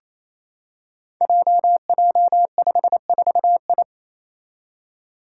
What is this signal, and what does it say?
Morse code, heard as a keyed tone, about 28 wpm, 700 Hz. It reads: JJ54I
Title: Five Challenging Letter-Number Combinations 28wpm